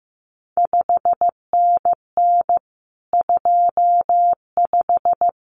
Morse code, heard as a keyed tone, about 15 wpm, 700 Hz. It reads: 5NN 25
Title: CQ WW DX Contest Exchange Encore - 15wpm